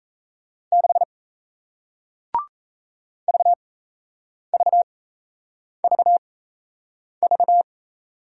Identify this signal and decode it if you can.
Morse code, heard as a keyed tone, about 38 wpm, spaced as for 8 wpm, 700 Hz. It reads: 6 VVVV